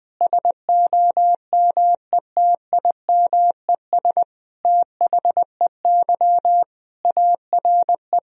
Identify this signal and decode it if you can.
Morse code, heard as a keyed tone, about 21 wpm, 700 Hz. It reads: SOMETIMES THEY ARE
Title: Sentences from Top 1000 Words - Review Part 3 - 20wpm